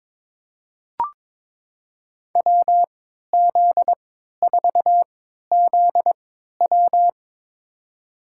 Morse code, heard as a keyed tone, about 22 wpm, spaced as for 12 wpm, 700 Hz. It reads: WZ4ZW
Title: USA Call Signs - 5 Characters - 3x Character Spacing - 22wpm